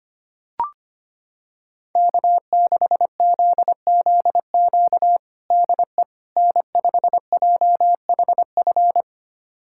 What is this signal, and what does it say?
Morse code, heard as a keyed tone, about 25 wpm, 700 Hz. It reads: K6ZZQ DE N5JHF
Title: USA Callsign DE Callsign - 5 Characters - 25wpm